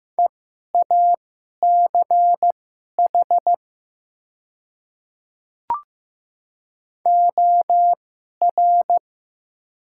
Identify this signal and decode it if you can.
Morse code, heard as a keyed tone, about 15 wpm, spaced as for 11 wpm, 700 Hz. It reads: EACH OR
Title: 2x Character Spacing - Top 200 Words - 15wpm